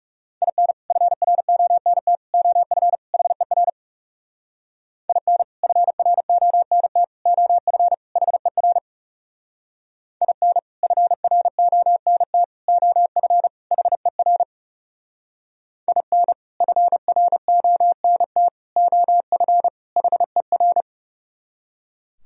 Morse code, heard as a keyed tone, about 38 wpm, 700 Hz. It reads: IN FRONT OF HER IN FRONT OF HER IN FRONT OF HER IN FRONT OF HER